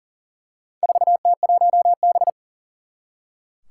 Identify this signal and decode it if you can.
Morse code, heard as a keyed tone, about 40 wpm, 700 Hz. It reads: 4T1B
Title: Four Challenging Letter-Number Combinations 40wpm